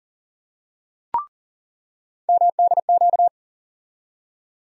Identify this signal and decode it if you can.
Morse code, heard as a keyed tone, about 40 wpm, 700 Hz. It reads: MDQ
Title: USA Call Sign Postfixes - 3 Letters 40wpm